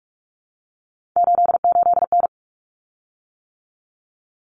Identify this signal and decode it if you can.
Morse code, heard as a keyed tone, about 44 wpm, 700 Hz. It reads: 88N
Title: Common Cut Numbers - 3-Digit 45wpm